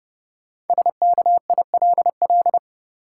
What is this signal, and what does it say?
Morse code, heard as a keyed tone, about 30 wpm, 700 Hz. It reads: SKILL
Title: Sentences from Top 1200 Words - New Word Drill - 30wpm